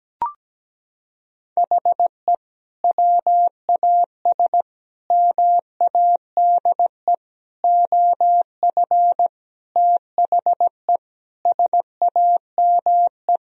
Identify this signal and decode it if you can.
Morse code, heard as a keyed tone, about 17 wpm, 700 Hz. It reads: HE WAS MADE OF THE SAME
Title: Sentences from Top 1000 Words - Review Part 4 - 17wpm